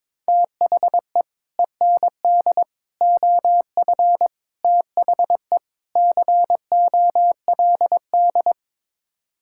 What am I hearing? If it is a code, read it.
Morse code, heard as a keyed tone, about 21 wpm, 700 Hz. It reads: THE END OF THE COLD
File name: practice / 22wpm / Build A Phrase - 5-Word Phrases - 22wpm